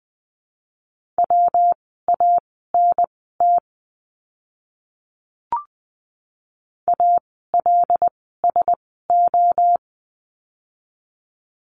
Morse code, heard as a keyed tone, about 20 wpm, spaced as for 15 wpm, 700 Hz. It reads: WANT ALSO
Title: 146 Lesson 24f - Letter y - Words 20wpm